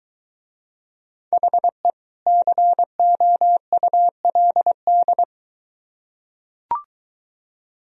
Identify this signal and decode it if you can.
Morse code, heard as a keyed tone, about 22 wpm, 700 Hz. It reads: HE COULD